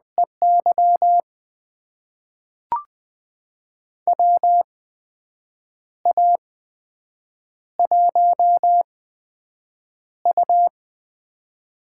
Morse code, heard as a keyed tone, about 21 wpm, spaced as for 7 wpm, 700 Hz. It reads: EY WA1U
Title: USA Call Signs - 4 Characters - 8x Character Spacing - 20wpm